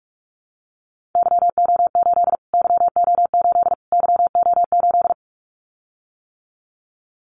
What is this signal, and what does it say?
Morse code, heard as a keyed tone, about 46 wpm, 700 Hz. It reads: YO8 YO8 YO8
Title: Callsign Prefixes for Top 25 DXCC - Repeated 3x - 45wpm